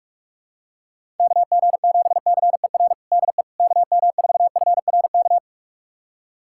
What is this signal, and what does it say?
Morse code, heard as a keyed tone, about 46 wpm, 700 Hz. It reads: KG7CER DE KM4FRK
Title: USA Callsign DE Callsign 45wpm